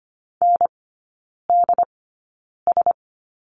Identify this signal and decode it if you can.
Morse code, heard as a keyed tone, about 25 wpm, spaced as for 9 wpm, 690 Hz. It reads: NDS
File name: Lesson 14e - Number 2 - Three Characters 25wpm